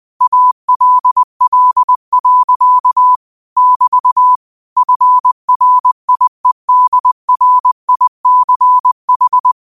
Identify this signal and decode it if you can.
Morse code, heard as a keyed tone, about 20 wpm, 1000 Hz. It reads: ALL. <BT> FRIEDRICH
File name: qotdfiles / 20wpm